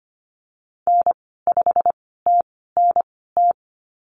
Morse code, heard as a keyed tone, about 25 wpm, spaced as for 16 wpm, 700 Hz. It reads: N5TNT
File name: Lesson 15g - Number 5 - Call Signs 25wpm